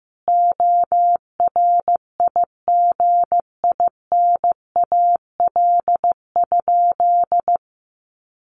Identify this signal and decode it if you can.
Morse code, heard as a keyed tone, about 15 wpm, 700 Hz. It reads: ORIGINAL?
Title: Sentences from Top 900 Words 15wpm